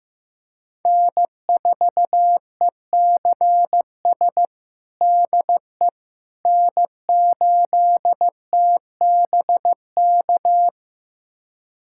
Morse code, heard as a keyed tone, about 15 wpm, 700 Hz. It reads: N4ECS DE N8TBK
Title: USA Callsign DE Callsign - 5 Characters - 15wpm